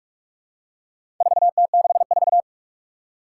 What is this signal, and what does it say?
Morse code, heard as a keyed tone, about 46 wpm, 700 Hz. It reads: 4T64
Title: Four Challenging Letter-Number Combinations 45wpm